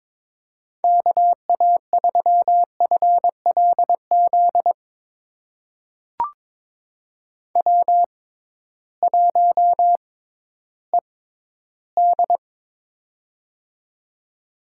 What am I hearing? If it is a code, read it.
Morse code, heard as a keyed tone, about 22 wpm, spaced as for 13 wpm, 700 Hz. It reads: KA3FLZ W1ED
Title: USA Call Signs - 6x Character Spacing - 22wpm